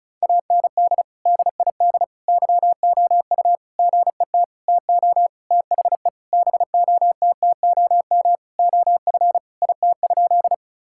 Morse code, heard as a keyed tone, about 36 wpm, 700 Hz. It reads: AND DID YOU GET TO THE BOTTOM OF IT?
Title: Sentences from Top 1000 Words - Review Part 6 - 35wpm